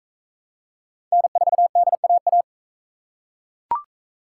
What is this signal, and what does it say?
Morse code, heard as a keyed tone, about 42 wpm, 700 Hz. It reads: N4DAA